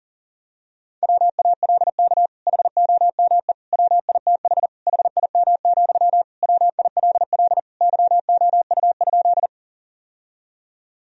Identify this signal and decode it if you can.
Morse code, heard as a keyed tone, about 40 wpm, 700 Hz. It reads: WALK HOME WITH HIM, WILL YOU?
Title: Sentences from Top 1000 Words - Review Part 1 - 40wpm